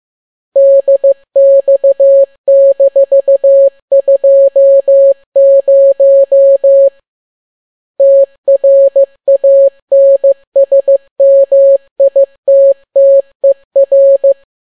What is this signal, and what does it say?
Morse code, heard as a keyed tone, about 15 wpm, 550 Hz. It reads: DX-20 TRANSMITTER
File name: n6tt p1 15C13